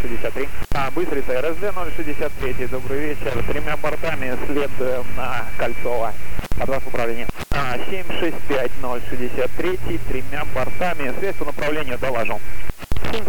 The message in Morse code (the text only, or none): none